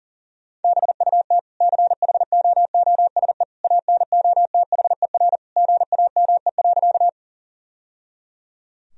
Morse code, heard as a keyed tone, about 40 wpm, 700 Hz. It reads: BUT CHOOSE ANOTHER GAME.
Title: Sentences from Top 1000 Words - Review Part 4 - 40wpm